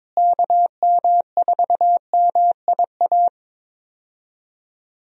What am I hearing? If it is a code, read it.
Morse code, heard as a keyed tone, about 22 wpm, 700 Hz. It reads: KM4MIA
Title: USA Call Signs - 6 Characters - Repeated 3x - 22wpm